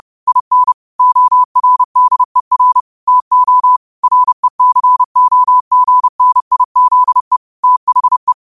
Morse code, heard as a keyed tone, about 30 wpm, 1000 Hz. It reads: IN ORDER TO RECOGNIZE THE
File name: qotdfiles / 30wpm